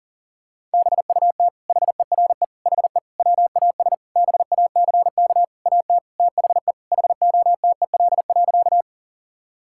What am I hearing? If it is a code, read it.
Morse code, heard as a keyed tone, about 40 wpm, 700 Hz. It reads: BUT HERE HE WAS BACK AT THE HOTEL.
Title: Sentences from Top 1600 Words 40wpm